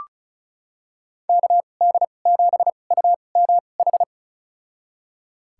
Morse code, heard as a keyed tone, about 34 wpm, spaced as for 25 wpm, 700 Hz. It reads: KD7UMH
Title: USA Call Signs - 6 Characters - 2x Character Spacing - 35wpm